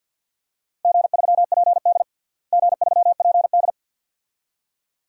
Morse code, heard as a keyed tone, about 45 wpm, 700 Hz. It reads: G3PD G3PD
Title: DXCC-100 Call Signs - 4 Characters - Repeated 2x - 50wpm